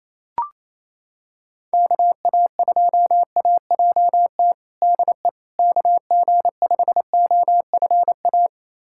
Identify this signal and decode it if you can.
Morse code, heard as a keyed tone, about 28 wpm, 700 Hz. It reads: KA2AJT DE KG5OFA
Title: USA Callsign DE Callsign - 6 Characters - 28wpm